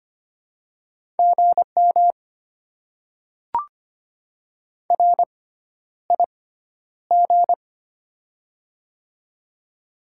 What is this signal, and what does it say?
Morse code, heard as a keyed tone, about 25 wpm, spaced as for 11 wpm, 700 Hz. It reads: GM RIG
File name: Top 20 QSO Elements - 6x Character Spacing - 25wpm